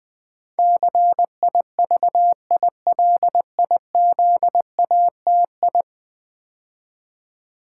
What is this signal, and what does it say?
Morse code, heard as a keyed tone, about 20 wpm, 700 Hz. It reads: CIVILIZATI
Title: Build A Word - 5 Letters At A Time - 12-Letter Words - 20wpm